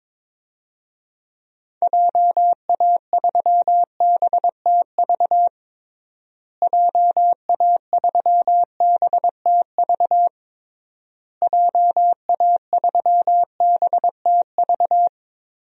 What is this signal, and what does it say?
Morse code, heard as a keyed tone, about 22 wpm, 700 Hz. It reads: JA3BTV JA3BTV JA3BTV
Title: DXCC-100 Call Signs - 6 Characters - Repeated 3x - 22wpm